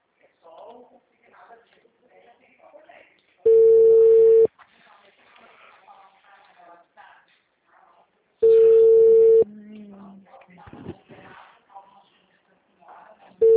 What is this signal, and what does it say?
Ringback tone (China, 450 Hz), with speech.